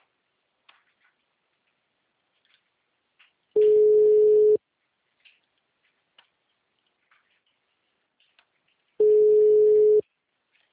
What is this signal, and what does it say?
Ringback tone (Europe, 425 Hz).
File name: out- -2014- - - .99